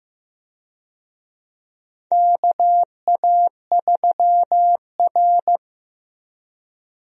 Morse code, heard as a keyed tone, about 15 wpm, 700 Hz. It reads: KA3R
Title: USA Call Signs - 4 Characters - 15wpm